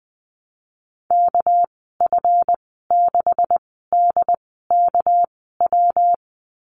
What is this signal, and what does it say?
Morse code, heard as a keyed tone, about 19 wpm, spaced as for 14 wpm, 700 Hz. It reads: KF6DKW
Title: 238 Lesson 38g - Letter q - Call Signs 20wpm